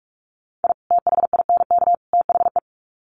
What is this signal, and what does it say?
Morse code, heard as a keyed tone, about 44 wpm, 700 Hz. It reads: I THINK THE
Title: Sets of 3 Words 45wpm